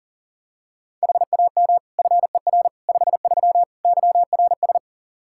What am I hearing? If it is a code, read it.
Morse code, heard as a keyed tone, about 40 wpm, 690 Hz. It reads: HAM FER 53 YRS
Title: Practice QSO - Exchange 3 - More Info - 40wpm